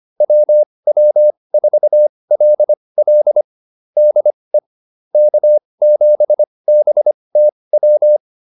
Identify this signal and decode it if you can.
Morse code, heard as a keyed tone, about 25 wpm, spaced as for 20 wpm, 600 Hz. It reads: WW4LL DE K7BTW